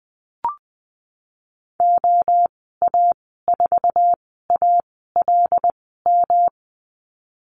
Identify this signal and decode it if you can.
Morse code, heard as a keyed tone, about 20 wpm, spaced as for 15 wpm, 700 Hz. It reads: OA4ALM
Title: 173 Lesson 28g - Number 9 - Call Signs 20wpm